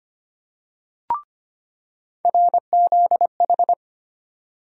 Morse code, heard as a keyed tone, about 25 wpm, 700 Hz. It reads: RZH
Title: USA Call Sign Postfixes - 3 Letters 25wpm